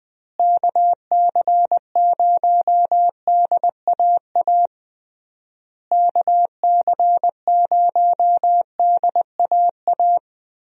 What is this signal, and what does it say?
Morse code, heard as a keyed tone, about 20 wpm, 700 Hz. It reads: KC0DAA KC0DAA
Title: USA Call Signs - 6 Characters - Repeated 3x - 20wpm